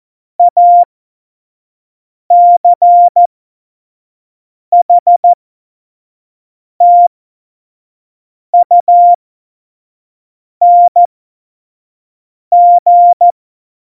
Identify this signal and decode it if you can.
Morse code, heard as a keyed tone, about 14 wpm, spaced as for 5 wpm, 700 Hz. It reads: ACHTUNG
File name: Morsekurs / Witze / Witz 26. September 17, Tempo 25